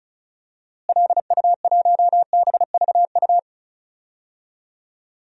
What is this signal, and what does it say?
Morse code, heard as a keyed tone, about 36 wpm, 700 Hz. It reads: LU1BVU
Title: DXCC-100 Call Signs - 6 Characters - Repeated 3x - 35wpm